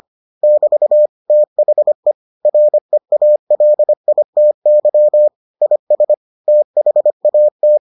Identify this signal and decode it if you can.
Morse code, heard as a keyed tone, about 25 wpm, 600 Hz. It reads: <BT> THE REALITY IS THAT